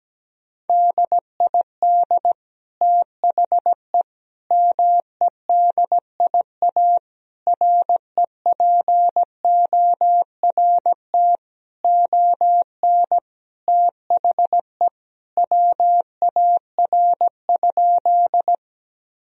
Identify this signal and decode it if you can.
Morse code, heard as a keyed tone, about 17 wpm, 700 Hz. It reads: DID THE MEDIA REPORT ON THE WAR?
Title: Sentences from Top 1000 Words - Review Part 4 - 17wpm